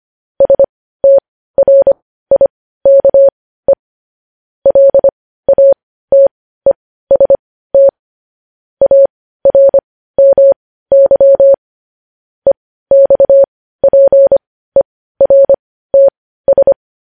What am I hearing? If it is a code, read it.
Morse code, heard as a keyed tone, about 25 wpm, spaced as for 15 wpm, 550 Hz. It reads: STRIKE LATEST ARMY EXPERTS
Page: News Headlines in Morse Code at 15 WPM